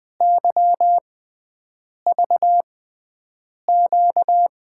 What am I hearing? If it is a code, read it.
Morse code, heard as a keyed tone, about 20 wpm, spaced as for 7 wpm, 700 Hz. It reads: YVQ
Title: USA Call Signs - 5 Characters - 6x Character Spacing - 20wpm